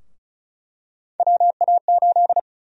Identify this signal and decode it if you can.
Morse code, heard as a keyed tone, about 34 wpm, 700 Hz. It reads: WA8
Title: USA Call Sign Prefixes - 35wpm